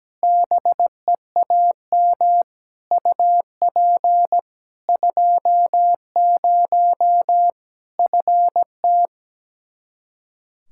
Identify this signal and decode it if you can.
Morse code, heard as a keyed tone, about 17 wpm, 700 Hz. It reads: BEAM UP 20 FT
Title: Practice QSO - Exchange 2 - Info - 17wpm